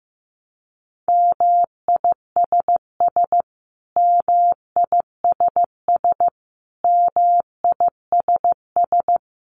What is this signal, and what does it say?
Morse code, heard as a keyed tone, about 15 wpm, 700 Hz. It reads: MISS MISS MISS
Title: Top 500 Words Repeated 15wpm